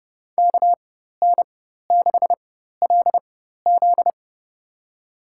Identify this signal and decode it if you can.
Morse code, heard as a keyed tone, about 30 wpm, spaced as for 14 wpm, 700 Hz. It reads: KN6LZ
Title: USA Call Signs - 4x Character Spacing - 30wpm